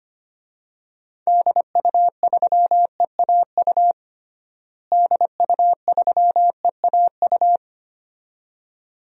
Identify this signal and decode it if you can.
Morse code, heard as a keyed tone, about 25 wpm, 700 Hz. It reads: DU3EAU DU3EAU
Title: DXCC-100 Call Signs - 6 Characters - Repeated 2x - 25wpm